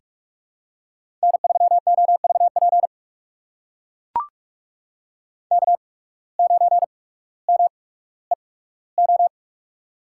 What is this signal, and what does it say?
Morse code, heard as a keyed tone, about 43 wpm, spaced as for 19 wpm, 700 Hz. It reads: N3OVP K9MEO
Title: USA Call Signs - 5 Characters - 8x Character Spacing - 45wpm